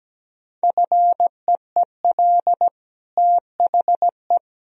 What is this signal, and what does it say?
Morse code, heard as a keyed tone, about 17 wpm, 700 Hz. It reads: FEEL THE
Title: Sentences from Top 1000 Words - Review Part 2 - 17wpm